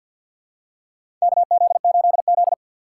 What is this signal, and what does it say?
Morse code, heard as a keyed tone, about 46 wpm, 700 Hz. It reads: KZ8Z
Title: USA Call Signs - 4 Characters - 3x Character Spacing - 50wpm